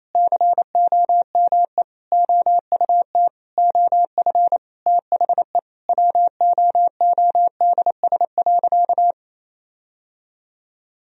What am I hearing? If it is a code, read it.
Morse code, heard as a keyed tone, about 28 wpm, 700 Hz. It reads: COME OUT OF THE WOODS.
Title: Sentences from Top 1000 Words - Review Part 2 - 28wpm